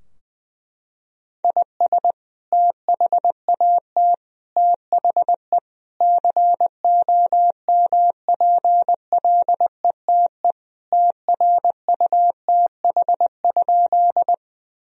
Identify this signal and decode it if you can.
Morse code, heard as a keyed tone, about 20 wpm, 700 Hz. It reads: IS THAT THE COMPLETE TRUTH?
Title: Sentences from Top 1000 Words - Review Part 2 - 20wpm